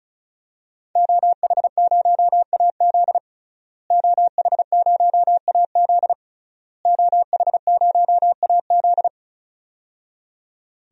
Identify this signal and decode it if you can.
Morse code, heard as a keyed tone, about 36 wpm, 700 Hz. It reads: OH0AZ OH0AZ OH0AZ
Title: DXCC-100 Call Signs - 5 Characters - Repeated 3x - 35wpm